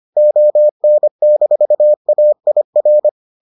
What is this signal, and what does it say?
Morse code, heard as a keyed tone, about 25 wpm, 600 Hz. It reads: ON-AIR